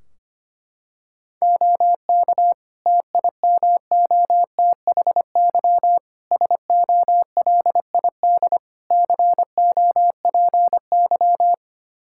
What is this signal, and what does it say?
Morse code, heard as a keyed tone, about 25 wpm, 700 Hz. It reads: OK TIMOTHY SOLID COPY
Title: K1USN SST QSO Practice 25wpm